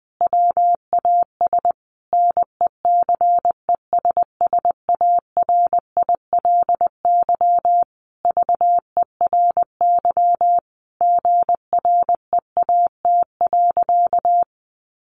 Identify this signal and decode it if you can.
Morse code, heard as a keyed tone, about 20 wpm, 700 Hz. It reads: WAS NECESSARILY VERY GREAT.
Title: Sentences from Top 1400 Words 20wpm